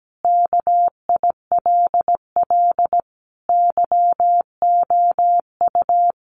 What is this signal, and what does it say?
Morse code, heard as a keyed tone, about 17 wpm, 700 Hz. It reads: KILL YOU
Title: A Sound Of Thunder 17wpm